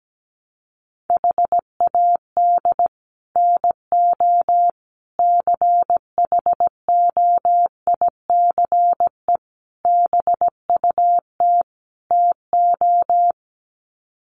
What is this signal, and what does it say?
Morse code, heard as a keyed tone, about 17 wpm, 700 Hz. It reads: HAD NO CHOICE BUT TO
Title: Sets of 5 Words 17wpm